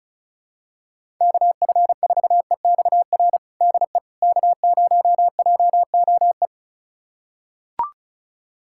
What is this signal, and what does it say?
Morse code, heard as a keyed tone, about 36 wpm, 700 Hz. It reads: KF4EXR DE K0JOE
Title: USA Callsign DE Callsign 35wpm